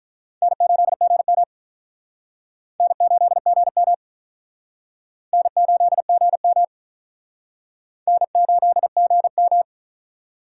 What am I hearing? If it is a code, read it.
Morse code, heard as a keyed tone, about 48 wpm, 700 Hz. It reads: N8GM N8GM N8GM N8GM